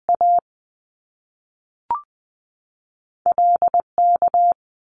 Morse code, heard as a keyed tone, about 20 wpm, 700 Hz. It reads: A L K
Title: Top Two Letter Combinations in Words 20wpm